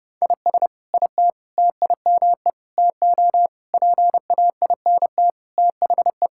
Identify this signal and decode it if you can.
Morse code, heard as a keyed tone, about 30 wpm, 700 Hz. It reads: IS IT TIME TO PAINT THE
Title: Sentences from Top 1000 Words - Review Part 6 - 30wpm